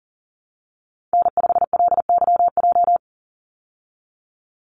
Morse code, heard as a keyed tone, about 40 wpm, 700 Hz. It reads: N5LYJ
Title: Call Signs Repeated 3x - 40wpm